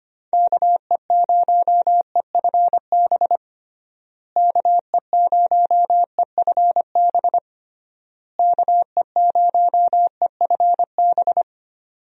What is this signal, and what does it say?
Morse code, heard as a keyed tone, about 25 wpm, 700 Hz. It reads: KE0EFB KE0EFB KE0EFB
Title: USA Call Signs - 6 Characters - Repeated 3x - 25wpm